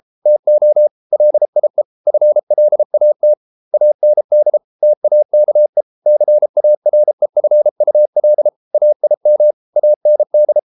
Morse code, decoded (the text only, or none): TO LIE FLAT AND TAKE CAREFUL AIM AND